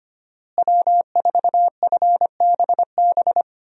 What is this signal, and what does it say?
Morse code, heard as a keyed tone, about 25 wpm, 700 Hz. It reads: W4FBB
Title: USA Call Signs - 5 Characters - Repeated 2x - 25wpm